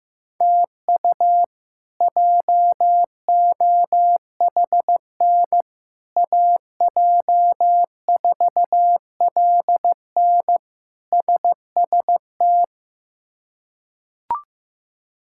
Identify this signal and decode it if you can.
Morse code, heard as a keyed tone, about 15 wpm, 700 Hz. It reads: TU JOHN AJ4LN SST
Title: K1USN SST QSO Practice - 15wpm